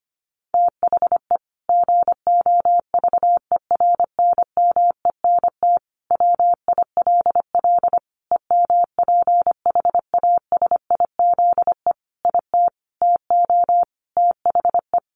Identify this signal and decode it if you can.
Morse code, heard as a keyed tone, about 25 wpm, 700 Hz. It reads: THE GOVERNMENT WILL EMPHASIZE IT TO THE
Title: Sentences from Top 1500 Words 25wpm